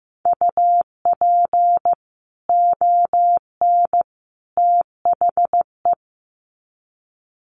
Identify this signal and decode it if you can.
Morse code, heard as a keyed tone, about 15 wpm, 700 Hz. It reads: UP ON THE
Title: Sets of 3 Words 15wpm